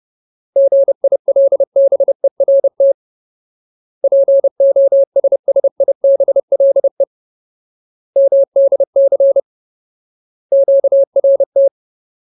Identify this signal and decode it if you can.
Morse code, heard as a keyed tone, about 30 wpm, 550 Hz. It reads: GILBERT POSSIBLE MDC QRT
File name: Ultimate Word List / 30WPM Ultimate